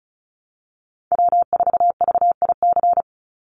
Morse code, heard as a keyed tone, about 34 wpm, 700 Hz. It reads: W4VIC
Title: Call Signs Repeated 3x - 35wpm